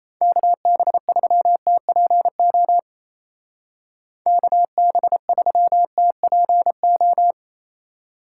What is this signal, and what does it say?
Morse code, heard as a keyed tone, about 35 wpm, 700 Hz. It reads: KB3TPO KB3TPO